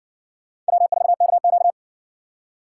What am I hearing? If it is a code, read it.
Morse code, heard as a keyed tone, about 60 wpm, 700 Hz. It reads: W4KQ